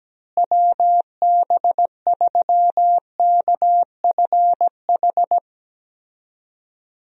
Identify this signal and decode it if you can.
Morse code, heard as a keyed tone, about 17 wpm, 700 Hz. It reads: WB3KFH
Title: USA Call Signs - 6 Characters - Repeated 2x - 17wpm